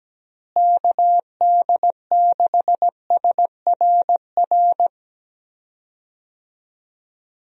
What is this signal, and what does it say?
Morse code, heard as a keyed tone, about 17 wpm, 700 Hz. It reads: KD6SRR
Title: USA Call Signs - 6 Characters - Repeated 3x - 17wpm